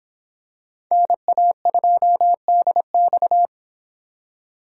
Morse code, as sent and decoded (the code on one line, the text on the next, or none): -. .- ..--- -.. -..-
NA2DX